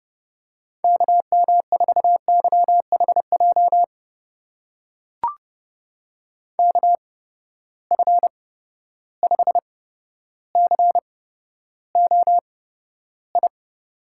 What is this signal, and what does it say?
Morse code, heard as a keyed tone, about 30 wpm, spaced as for 13 wpm, 700 Hz. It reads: KM4YHJ KF5COI
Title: USA Call Signs - 6 Characters - 8x Character Spacing - 30wpm